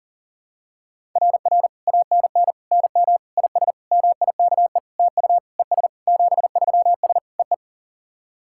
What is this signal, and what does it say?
Morse code, heard as a keyed tone, about 40 wpm, 700 Hz. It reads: RR ANN NM IS MIKE TU ES 73S EE
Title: SOTA QSO Practice 40wpm